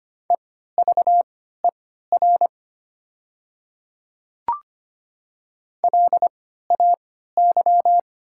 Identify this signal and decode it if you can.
Morse code, heard as a keyed tone, about 25 wpm, spaced as for 14 wpm, 690 Hz. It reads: EVER LAY
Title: 3x Character Spacing - Top 500 Words - 25wpm